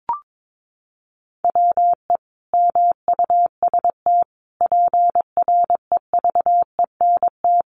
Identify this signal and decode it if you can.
Morse code, heard as a keyed tone, about 22 wpm, 700 Hz. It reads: WE MUST PREVENT
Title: Sentences from Top 1100 Words 22wpm